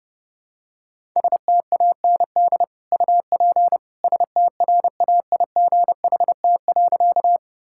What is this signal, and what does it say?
Morse code, heard as a keyed tone, about 30 wpm, 700 Hz. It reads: STAND UP STRAIGHT.
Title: Sentences from Top 1000 Words - Review Part 6 - 30wpm